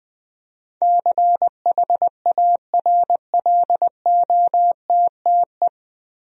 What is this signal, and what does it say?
Morse code, heard as a keyed tone, about 20 wpm, 700 Hz. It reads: CHARLOTTE
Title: USA Common Names - 20wpm